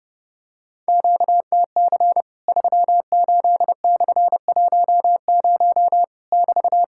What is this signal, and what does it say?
Morse code, heard as a keyed tone, about 30 wpm, 700 Hz. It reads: QTC 38/10 <BT>
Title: Worked All Europe DX Contest - QTC Exchange - 30wpm